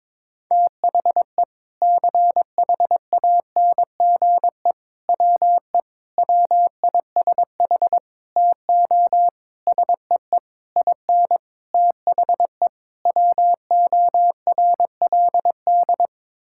Morse code, heard as a keyed tone, about 22 wpm, 700 Hz. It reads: THE CHANGE WE WISH TO SEE IN THE WORLD
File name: Sets of 9 Words - 22wpm